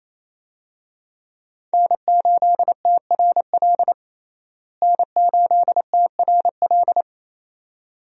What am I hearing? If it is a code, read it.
Morse code, heard as a keyed tone, about 28 wpm, 700 Hz. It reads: N8TRL N8TRL
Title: USA Call Signs - 5 Characters - Repeated 2x - 28wpm